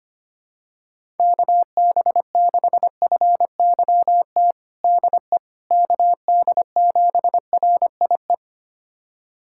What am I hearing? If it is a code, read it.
Morse code, heard as a keyed tone, about 25 wpm, 700 Hz. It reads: KB6FYT DE KD7RIE
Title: USA Callsign DE Callsign - 6 Characters - 25wpm